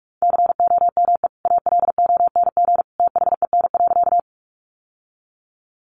Morse code, heard as a keyed tone, about 44 wpm, 700 Hz. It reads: COME ALONG THEN.
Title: Sentences from Top 700 Words 45wpm